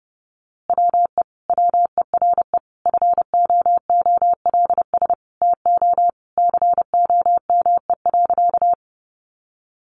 Morse code, heard as a keyed tone, about 30 wpm, 700 Hz. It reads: WE WERE FOOLS TO COME.
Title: A Sound Of Thunder 30wpm